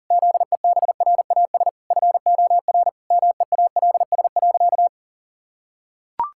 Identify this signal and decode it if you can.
Morse code, heard as a keyed tone, about 40 wpm, 700 Hz. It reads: ZEBRAS FOR MEALS.
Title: Pangram Challenge v2 - 40wpm